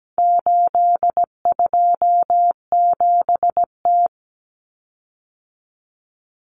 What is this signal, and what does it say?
Morse code, heard as a keyed tone, about 17 wpm, 700 Hz. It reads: 827T
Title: Common Cut Numbers - 4-Digit 17wpm